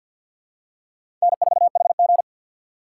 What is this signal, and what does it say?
Morse code, heard as a keyed tone, about 48 wpm, 700 Hz. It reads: N4HG
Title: USA Call Signs - 2x Character Spacing - 50wpm